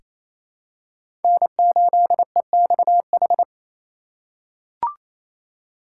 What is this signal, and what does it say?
Morse code, heard as a keyed tone, about 28 wpm, 700 Hz. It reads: N8EXH
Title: USA Call Signs - 5 Characters - 4x Character Spacing - 28wpm